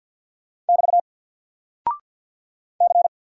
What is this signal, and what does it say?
Morse code, heard as a keyed tone, about 48 wpm, 700 Hz. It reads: <BT> C
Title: Single Character (Letter:Number:Punctuation:Prosign) Alt 50wpm